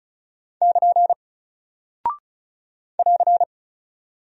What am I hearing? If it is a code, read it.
Morse code, heard as a keyed tone, about 35 wpm, 700 Hz. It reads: <KN> <AR>
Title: Single Character (Letter:Number:Punctuation:Prosign) 35wpm